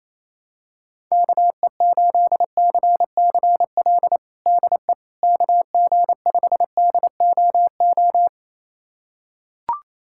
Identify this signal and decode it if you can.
Morse code, heard as a keyed tone, about 28 wpm, 700 Hz. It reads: KE8CCL DE KG5DOO
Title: USA Callsign DE Callsign - 6 Characters - 28wpm